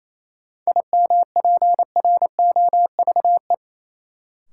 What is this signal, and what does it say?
Morse code, heard as a keyed tone, about 28 wpm, 700 Hz. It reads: IMPROVE
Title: Sentences from Top 800 Words - New Word Drill - 28wpm